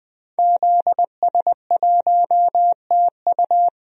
Morse code, heard as a keyed tone, about 20 wpm, 700 Hz. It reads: ZS1TU
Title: DXCC-100 Call Signs - 5 Characters - 20wpm